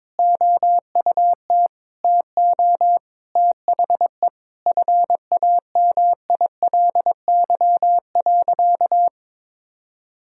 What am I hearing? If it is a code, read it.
Morse code, heard as a keyed tone, about 21 wpm, 700 Hz. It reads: OUT TO THE FAMILY.
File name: practice / 22wpm / Sentences with 8-Word Phrases - 22wpm